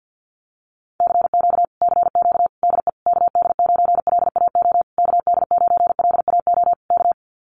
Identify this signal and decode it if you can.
Morse code, heard as a keyed tone, about 50 wpm, 700 Hz. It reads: CQ CQ DE KD9LAO KD9LAO K
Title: Common QSO Phrases Encore 50wpm